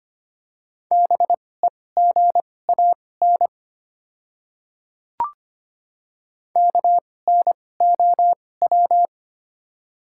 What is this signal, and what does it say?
Morse code, heard as a keyed tone, about 25 wpm, spaced as for 18 wpm, 700 Hz. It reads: BEGAN KNOW
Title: 2x Character Spacing - Top 500 Words - 25wpm